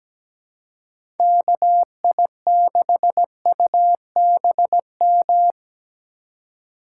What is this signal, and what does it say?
Morse code, heard as a keyed tone, about 20 wpm, 700 Hz. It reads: KI6UBM
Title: Speed Racing - USA 2x3 Call Signs 17wpm